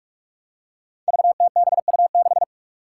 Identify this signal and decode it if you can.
Morse code, heard as a keyed tone, about 46 wpm, 700 Hz. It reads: VTBU6
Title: Five Challenging Letter-Number Combinations 45wpm